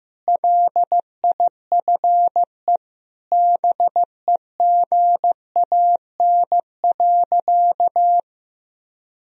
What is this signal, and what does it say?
Morse code, heard as a keyed tone, about 15 wpm, 700 Hz. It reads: LIFE BEGAN.
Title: Sentences from Top 1000 Words - Review Part 1 - 15wpm